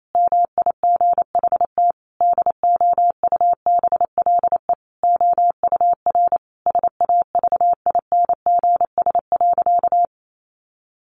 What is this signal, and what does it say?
Morse code, heard as a keyed tone, about 28 wpm, 700 Hz. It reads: MIGHT DOUBLE OUR SAVINGS.
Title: Sentences from Top 1400 Words 28wpm